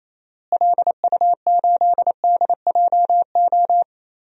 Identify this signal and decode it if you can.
Morse code, heard as a keyed tone, about 28 wpm, 700 Hz. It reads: LU8DJO
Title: DXCC-100 Call Signs - 6 Characters - Repeated 3x - 28wpm